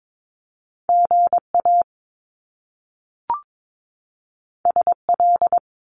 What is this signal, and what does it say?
Morse code, heard as a keyed tone, about 22 wpm, 700 Hz. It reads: GA SL
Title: Top Two Letter Combinations in Words 22wpm